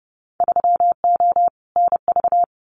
Morse code, heard as a keyed tone, about 30 wpm, 700 Hz. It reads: 3O NV
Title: Winter Field Day Exchanges Encore 30wpm